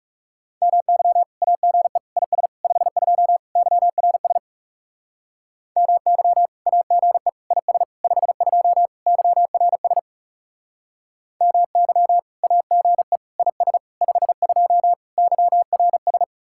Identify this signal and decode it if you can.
Morse code, heard as a keyed tone, about 37 wpm, 700 Hz. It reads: MY AGE IS 52 YRS MY AGE IS 52 YRS MY AGE IS 52 YRS
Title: Speed-Racing Common QSO Phrases Encore - 30wpm